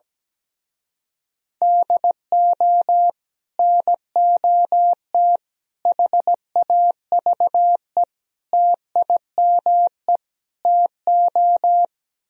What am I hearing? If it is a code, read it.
Morse code, heard as a keyed tone, about 17 wpm, 700 Hz. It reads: DO NOT HAVE TIME TO
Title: Speed-Racing - Sets of 5 Words 17wpm